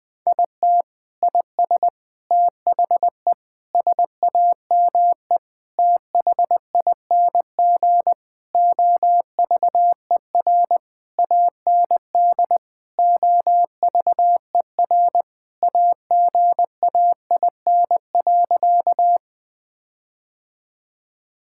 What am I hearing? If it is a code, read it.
Morse code, heard as a keyed tone, about 20 wpm, 700 Hz. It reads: IT IS THE SAME THING OVER AND OVER AGAIN.
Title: Sentences with 6-Word Phrases - 20wpm